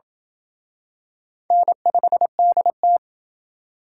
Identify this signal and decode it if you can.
Morse code, heard as a keyed tone, about 27 wpm, 700 Hz. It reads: N5DT